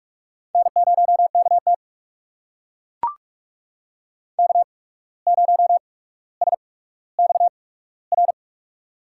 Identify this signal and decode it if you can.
Morse code, heard as a keyed tone, about 44 wpm, spaced as for 18 wpm, 700 Hz. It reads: N0KT K0SXR
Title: USA Call Signs - 8x Character Spacing - 45wpm